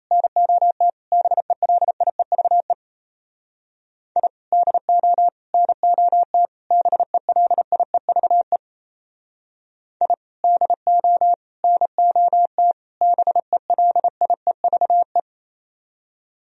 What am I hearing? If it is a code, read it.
Morse code, heard as a keyed tone, about 33 wpm, 700 Hz. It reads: NOT BELIEVE I DO NOT BELIEVE I DO NOT BELIEVE
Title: Speed-Racing - Sets of 4 Words 28wpm